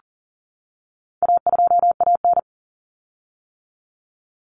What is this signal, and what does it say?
Morse code, heard as a keyed tone, about 40 wpm, 700 Hz. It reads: A2AN
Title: Common Cut Numbers - 4-Digit 40wpm